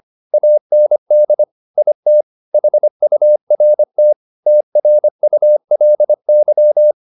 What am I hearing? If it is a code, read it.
Morse code, heard as a keyed tone, about 25 wpm, 600 Hz. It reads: AND IT HURT TRULY
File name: the old man and the sea part 1 25wpm